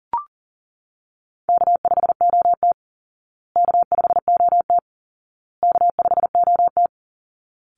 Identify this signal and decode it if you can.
Morse code, heard as a keyed tone, about 37 wpm, 700 Hz. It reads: K5OT K5OT K5OT
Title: Easy Call Signs Repeated 3x - 40wpm